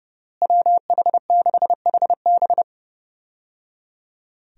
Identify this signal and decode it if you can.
Morse code, heard as a keyed tone, about 30 wpm, 700 Hz. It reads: WH6HB
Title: USA Call Signs - 5 Characters - Repeated 3x - 30wpm